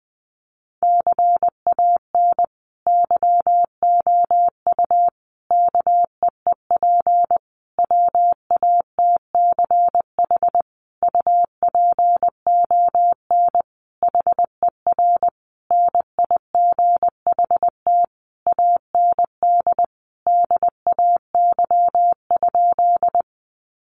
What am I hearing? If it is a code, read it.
Morse code, heard as a keyed tone, about 20 wpm, 700 Hz. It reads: CAN YOU KEEP WATCH UPON HER NIGHT AND DAY?
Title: Sentences from Top 1100 Words 20wpm